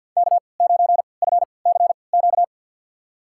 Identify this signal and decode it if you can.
Morse code, heard as a keyed tone, about 52 wpm, spaced as for 29 wpm, 700 Hz. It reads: K9FCQ
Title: USA Call Signs - 5 Characters - 3x Character Spacing - 50wpm